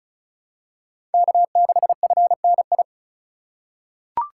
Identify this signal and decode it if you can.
Morse code, heard as a keyed tone, about 36 wpm, 690 Hz. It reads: K6FNI
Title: USA Call Signs - 3x Character Spacing - 35wpm